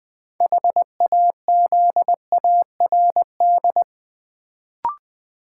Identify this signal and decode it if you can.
Morse code, heard as a keyed tone, about 20 wpm, 700 Hz. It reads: HAZARD
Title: Build A Word - 3 Letters At A Time - 6-Letter Words - 20wpm